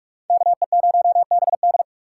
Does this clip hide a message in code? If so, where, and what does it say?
Yes: Morse code from 0.3 to 1.8 s, heard as a keyed tone, about 44 wpm, 700 Hz. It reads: KE0BD